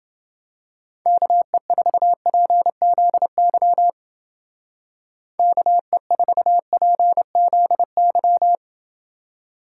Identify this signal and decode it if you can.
Morse code, heard as a keyed tone, about 28 wpm, 700 Hz. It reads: KE4PZY KE4PZY